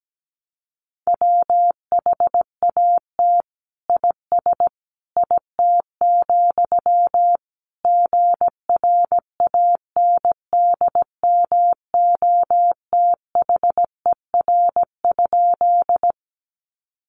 Morse code, heard as a keyed tone, about 17 wpm, 700 Hz. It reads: WHAT IS IT, GRANDMOTHER?
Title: Sentences from Top 1200 Words 17wpm